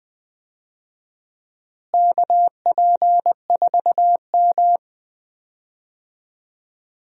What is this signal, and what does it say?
Morse code, heard as a keyed tone, about 20 wpm, 700 Hz. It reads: KP4M
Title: DXCC-100 Call Signs - 4 Characters - Repeated 2x - 20wpm